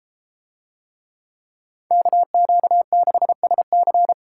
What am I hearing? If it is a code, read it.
Morse code, heard as a keyed tone, about 33 wpm, 700 Hz. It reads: KQ6SC